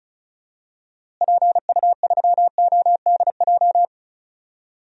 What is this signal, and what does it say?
Morse code, heard as a keyed tone, about 36 wpm, 700 Hz. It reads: PU3ODJ
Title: DXCC-100 Call Signs - 6 Characters - Repeated 3x - 35wpm